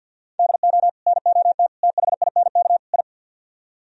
Morse code, heard as a keyed tone, about 66 wpm, spaced as for 58 wpm, 700 Hz. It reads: DO NOT THINK I
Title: Speed-Racing - Sets of 4 Words 50wpm